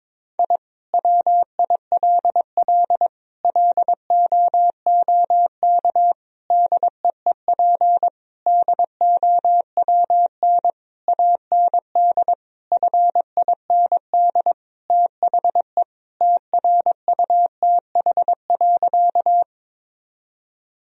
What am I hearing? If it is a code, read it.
Morse code, heard as a keyed tone, about 22 wpm, 700 Hz. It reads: I WILL LOOK DEEP DOWN AND FIND THE TRUTH.
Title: Sentences from Top 1000 Words - Review Part 3 - 22wpm